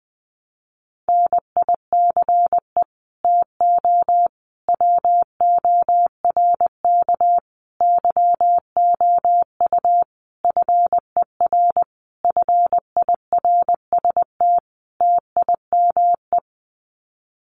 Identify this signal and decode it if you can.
Morse code, heard as a keyed tone, about 20 wpm, 700 Hz. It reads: NICE TO WORK YOU FER FIRST TIME
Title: Common QSO Phrases 20wpm